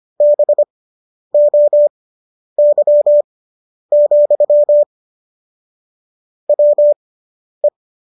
Morse code, heard as a keyed tone, about 25 wpm, spaced as for 10 wpm, 590 Hz. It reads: BOY, WE